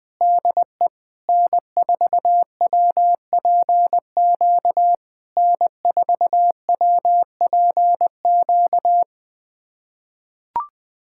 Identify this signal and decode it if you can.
Morse code, heard as a keyed tone, about 20 wpm, 690 Hz. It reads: DE N4WPQ N4WPQ
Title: SOTA QSO Practice 20wpm